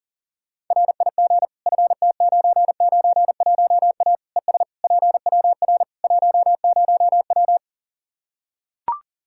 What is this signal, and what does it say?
Morse code, heard as a keyed tone, about 40 wpm, 700 Hz. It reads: RIG FT991A ES PWR 10W
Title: Practice QSO - Exchange 2 - Info - 40wpm